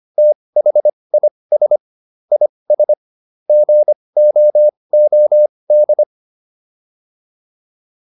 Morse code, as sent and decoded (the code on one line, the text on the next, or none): - .... .. ... / .. ... / --. --- --- -..
THIS IS GOOD